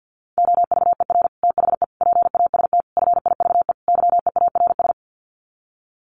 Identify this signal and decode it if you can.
Morse code, heard as a keyed tone, about 50 wpm, 700 Hz. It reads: OVER THE PAST FIVE YEARS
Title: Sets of 5 Words Encore 50wpm